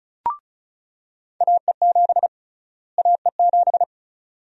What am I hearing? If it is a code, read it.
Morse code, heard as a keyed tone, about 34 wpm, 700 Hz. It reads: AE7 AE7
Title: USA Call Sign Prefixes - Repeated 2x - 35wpm